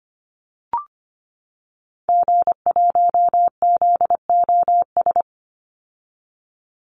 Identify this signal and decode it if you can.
Morse code, heard as a keyed tone, about 25 wpm, 700 Hz. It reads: G1ZOS
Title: Call Signs - 25wpm